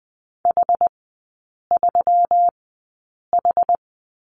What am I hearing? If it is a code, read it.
Morse code, heard as a keyed tone, about 20 wpm, spaced as for 8 wpm, 700 Hz. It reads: H3H
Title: 138 Lesson 23e - Letter f - Three Characters 20wpm